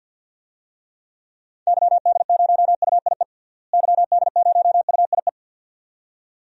Morse code, heard as a keyed tone, about 50 wpm, 700 Hz. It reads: YD0UIE YD0UIE
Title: DXCC-100 Call Signs - 6 Characters - Repeated 2x - 50wpm